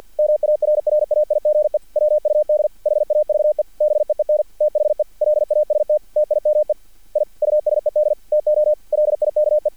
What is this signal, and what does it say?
Morse code, heard as a keyed tone, about 50 wpm, 600 Hz. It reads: MARRAIGE WUD HAVE BEEN THE LAST TIME I USED TO LIKE
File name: archived / CRSnet